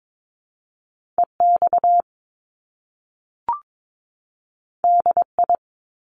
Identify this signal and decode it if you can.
Morse code, heard as a keyed tone, about 22 wpm, 700 Hz. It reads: EX DI
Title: Top Two Letter Combinations in Words 22wpm